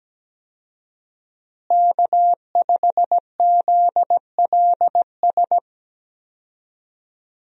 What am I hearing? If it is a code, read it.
Morse code, heard as a keyed tone, about 17 wpm, 700 Hz. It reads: K5ZLS
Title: USA Call Signs - 5 Characters - Repeated 3x - 17wpm